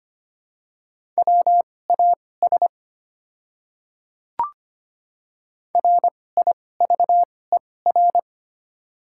Morse code, heard as a keyed tone, about 25 wpm, spaced as for 18 wpm, 700 Hz. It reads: WAS RIVER
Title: 2x Character Spacing - Top 300 Words - 25wpm